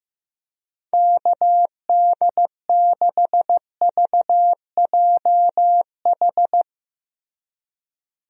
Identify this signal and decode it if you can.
Morse code, heard as a keyed tone, about 15 wpm, 700 Hz. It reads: KD6VJH
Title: USA Call Signs - 6 Characters - Repeated 2x - 15wpm